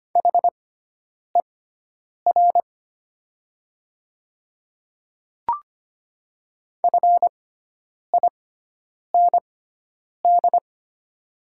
Morse code, heard as a keyed tone, about 25 wpm, spaced as for 9 wpm, 690 Hz. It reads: HER FIND
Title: 6x Character Spacing - Top 100 Words - 25wpm